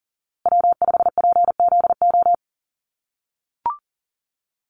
Morse code, heard as a keyed tone, about 40 wpm, 700 Hz. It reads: W5PZO
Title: USA Call Signs Repeated 2x 40wpm